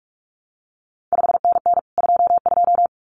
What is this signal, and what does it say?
Morse code, heard as a keyed tone, about 46 wpm, 700 Hz. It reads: 5NN 22
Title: CQ WW DX Contest Exchange Encore - 45wpm